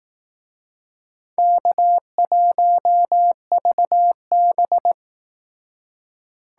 Morse code, heard as a keyed tone, about 18 wpm, 700 Hz. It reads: K1VB
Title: Speed Racing - USA 1x2 Call Signs 15wpm